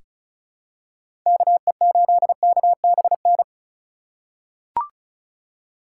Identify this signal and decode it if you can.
Morse code, heard as a keyed tone, about 34 wpm, 700 Hz. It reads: KE8KBN
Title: USA Call Signs - 2x Character Spacing - 35wpm